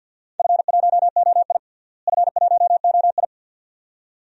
Morse code, heard as a keyed tone, about 49 wpm, 700 Hz. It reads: F1OI F1OI
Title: DXCC-100 Call Signs - 4 Characters - Repeated 2x - 50wpm